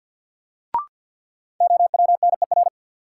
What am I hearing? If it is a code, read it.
Morse code, heard as a keyed tone, about 53 wpm, 700 Hz. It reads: OWNER
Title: Sentences from Top 1600 Words - New Word Drill Alt - 50wpm